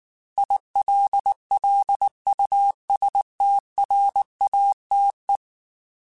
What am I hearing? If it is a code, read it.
Morse code, heard as a keyed tone, about 19 wpm, 800 Hz. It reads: ILLUSTRATE
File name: Technical Subjects / Technical Subjects-mp3s / AND Gate @18